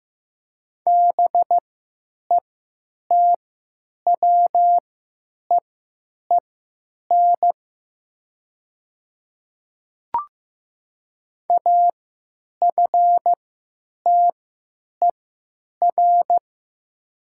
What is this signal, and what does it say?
Morse code, heard as a keyed tone, about 15 wpm, spaced as for 8 wpm, 700 Hz. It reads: BETWEEN AFTER
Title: 3x Character Spacing - Top 300 Words - 15wpm